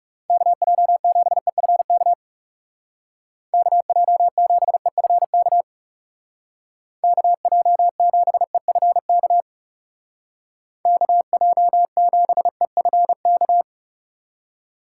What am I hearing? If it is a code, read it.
Morse code, heard as a keyed tone, about 38 wpm, 700 Hz. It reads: KJ7EFK KJ7EFK KJ7EFK KJ7EFK